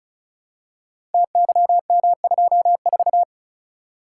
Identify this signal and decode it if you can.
Morse code, heard as a keyed tone, about 36 wpm, 700 Hz. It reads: TYM24
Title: Code Groups - 5 Letter:Numbers - 35wpm